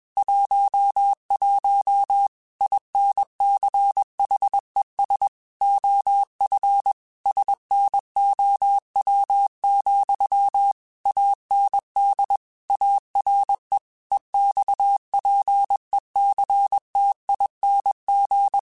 Morse code, heard as a keyed tone, about 21 wpm, 800 Hz. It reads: 11 INCHES OF SNOW, AND ARE EXPECTING
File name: QSOs / QSOs-mp3s / WA7QOX 1@20